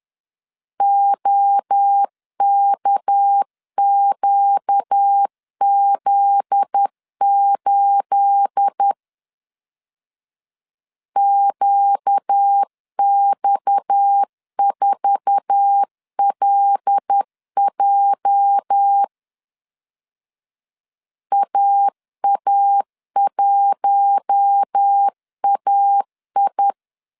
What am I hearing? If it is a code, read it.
Morse code, heard as a keyed tone, about 10 wpm, 800 Hz. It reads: OKQZ8 QX4LJ AA1AI